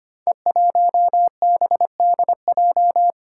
Morse code, heard as a keyed tone, about 25 wpm, 700 Hz. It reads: E1BDJ
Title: Code Groups - 5 Letter:Numbers - 25wpm